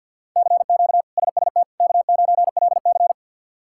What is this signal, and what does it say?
Morse code, heard as a keyed tone, about 53 wpm, 700 Hz. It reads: CQ SST K9LC
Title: K1USN SST QSO Practice - 50wpm